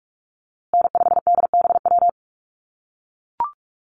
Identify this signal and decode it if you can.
Morse code, heard as a keyed tone, about 44 wpm, 700 Hz. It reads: N5DBW
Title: USA Call Signs Repeated 2x 45wpm